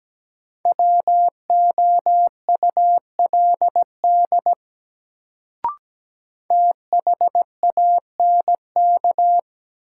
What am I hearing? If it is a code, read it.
Morse code, heard as a keyed tone, about 17 wpm, 700 Hz. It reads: WOULD THANK
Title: Sets of 2 Words - New Word Drill - 17wpm